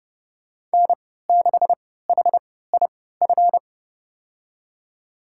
Morse code, heard as a keyed tone, about 31 wpm, spaced as for 17 wpm, 700 Hz. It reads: N6HIF
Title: USA Call Signs - 5 Characters - 3x Character Spacing - 30wpm